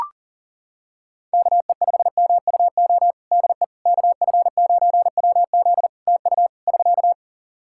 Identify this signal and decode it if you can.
Morse code, heard as a keyed tone, about 37 wpm, 700 Hz. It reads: KE5MUO DE KF9WZ TU <SK>
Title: K1USN SST QSO Practice 40wpm